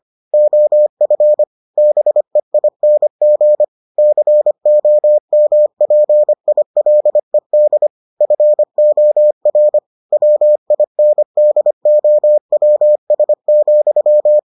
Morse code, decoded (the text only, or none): OF BEING COMPILED FOR WINDOWS,